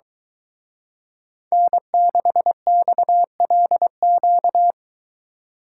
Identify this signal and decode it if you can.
Morse code, heard as a keyed tone, about 23 wpm, 700 Hz. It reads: N6XLQ